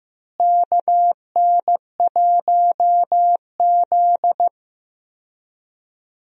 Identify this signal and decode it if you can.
Morse code, heard as a keyed tone, about 15 wpm, 700 Hz. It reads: KN1Z
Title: USA Call Signs - 4 Characters - Repeated 3x - 15wpm